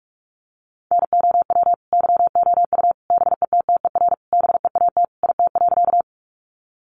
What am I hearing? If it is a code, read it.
Morse code, heard as a keyed tone, about 44 wpm, 700 Hz. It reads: NOW YOU BETTER BEAT IT.
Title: Sentences from Top 1300 Words 45wpm